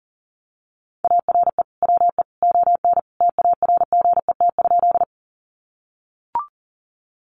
Morse code, heard as a keyed tone, about 40 wpm, 700 Hz. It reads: ARE WE ON TARGET?
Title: Sentences from Top 1200 Words 40wpm